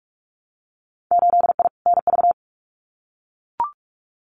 Morse code, heard as a keyed tone, about 44 wpm, 700 Hz. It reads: 8I NV
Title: Winter Field Day Exchanges Encore 45wpm